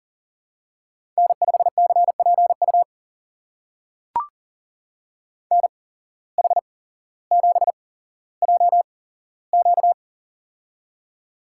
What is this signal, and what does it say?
Morse code, heard as a keyed tone, about 40 wpm, spaced as for 17 wpm, 700 Hz. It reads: N5CPU NH7JQ
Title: USA Call Signs - 8x Character Spacing - 40wpm